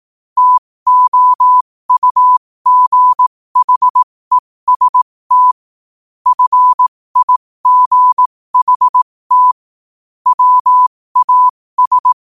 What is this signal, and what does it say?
Morse code, heard as a keyed tone, about 18 wpm, spaced as for 15 wpm, 1000 Hz. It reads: TOUGHEST FIGHT WAS
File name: qotdfiles / 15wpm